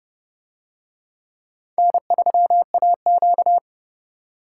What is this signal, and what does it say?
Morse code, heard as a keyed tone, about 29 wpm, 700 Hz. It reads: N3AQ